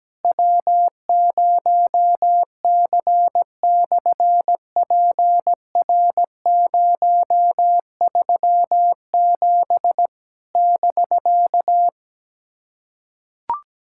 Morse code, heard as a keyed tone, about 17 wpm, 700 Hz. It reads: W0C/PR037 <BK>
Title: SOTA QSO Practice 17wpm